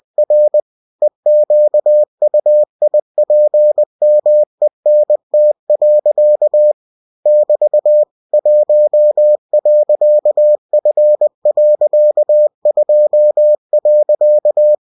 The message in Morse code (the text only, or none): R EQUIPMENT. <BT> 1.F.2.